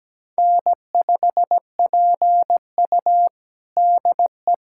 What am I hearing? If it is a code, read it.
Morse code, heard as a keyed tone, about 17 wpm, 700 Hz. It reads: N5PU DE
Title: USA Callsign DE Callsign - 4 Characters - 17wpm